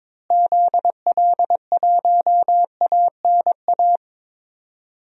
Morse code, heard as a keyed tone, about 22 wpm, 700 Hz. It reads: ZL1ANA
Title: DXCC-100 Call Signs - 6 Characters - Repeated 3x - 22wpm